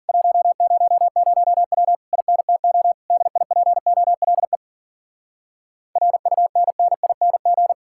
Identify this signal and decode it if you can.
Morse code, heard as a keyed tone, about 52 wpm, 700 Hz. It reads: 100W INTO DIPOLE RUNNING